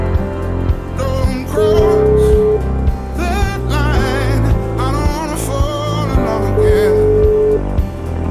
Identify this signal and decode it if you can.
Ringback tone (Europe, 425 Hz), with noise.